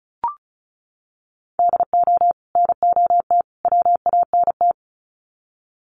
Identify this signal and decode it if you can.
Morse code, heard as a keyed tone, about 34 wpm, 700 Hz. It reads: DO NOT WANT
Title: Sets of 3 Words 35wpm